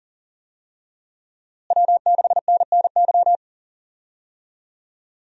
Morse code, heard as a keyed tone, about 40 wpm, 700 Hz. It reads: W6NNY
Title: USA Call Signs - 5 Characters - Repeated 3x - 40wpm